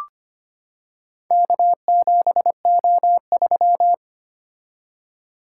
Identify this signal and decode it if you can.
Morse code, heard as a keyed tone, about 25 wpm, 700 Hz. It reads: K7O3
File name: practice / 25wpm / Four Letter-Number Combinations - 25wpm